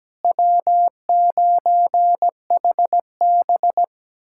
Morse code, heard as a keyed tone, about 17 wpm, 700 Hz. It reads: W9HB
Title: USA Callsign DE Callsign - 4 Characters - 17wpm